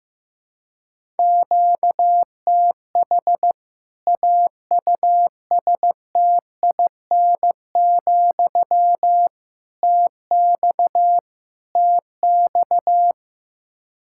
Morse code, heard as a keyed tone, about 15 wpm, 700 Hz. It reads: QTH AUSTIN, TX TX
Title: K1USN SST QSO Practice 15wpm